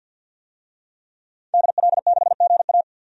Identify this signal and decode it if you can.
Morse code, heard as a keyed tone, about 47 wpm, 700 Hz. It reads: DL6GA
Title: DXCC-100 Call Signs - 5 Characters - Repeated 2x - 50wpm